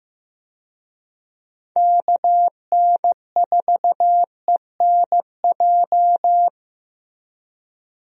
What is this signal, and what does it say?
Morse code, heard as a keyed tone, about 15 wpm, 700 Hz. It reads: KN4ENJ
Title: USA Call Signs - 6 Characters - Repeated 2x - 15wpm